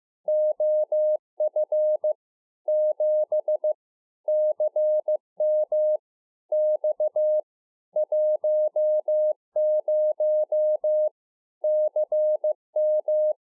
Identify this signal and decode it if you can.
Morse code, heard as a keyed tone, about 15 wpm, 610 Hz. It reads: OF 7 CM X 10 CM
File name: 2011-04-mp3 / A SMALL TRANCEIVER 2 15WPM